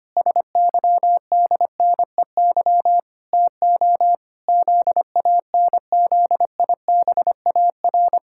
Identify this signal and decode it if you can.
Morse code, heard as a keyed tone, about 25 wpm, 700 Hz. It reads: SYDNEY TO ZANZIBAR
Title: Pangram Challenge v2 - 25wpm